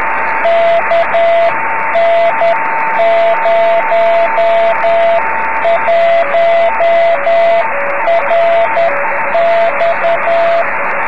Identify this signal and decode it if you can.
Morse code, heard as a keyed tone, about 10 wpm, 680 Hz, with radio noise. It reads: KN01RX
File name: om3pv 70mhz / zaznam / z35msk